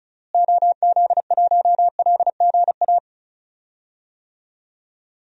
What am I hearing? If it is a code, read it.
Morse code, heard as a keyed tone, about 34 wpm, 700 Hz. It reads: OZ1LGA
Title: DXCC-100 Call Signs - 6 Characters - Repeated 2x - 35wpm